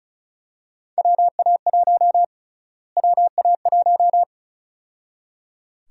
Morse code, heard as a keyed tone, about 34 wpm, 700 Hz. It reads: WA1 WA1
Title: USA Call Sign Prefixes - Repeated 2x - 35wpm